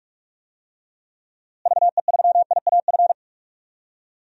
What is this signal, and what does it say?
Morse code, heard as a keyed tone, about 47 wpm, 700 Hz. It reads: VE3IAF
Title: DXCC-100 Call Signs - 6 Characters - Repeated 2x - 45wpm